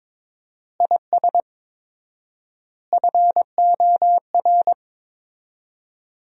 Morse code, heard as a keyed tone, about 22 wpm, 700 Hz. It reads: IS FOR
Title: 4x Word-Spacing - Sentences from Top 400 Words - 22wpm